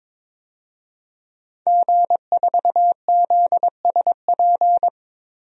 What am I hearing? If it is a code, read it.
Morse code, heard as a keyed tone, about 21 wpm, 700 Hz. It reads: G4ZSP
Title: DXCC-100 Call Signs - 5 Characters - 22wpm